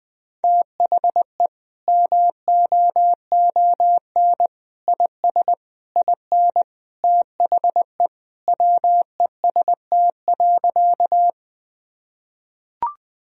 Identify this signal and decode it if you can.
Morse code, heard as a keyed tone, about 20 wpm, 700 Hz. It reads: THE MOON IS IN THE WEST.
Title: Sentences from Top 1000 Words - Review Part 3 - 20wpm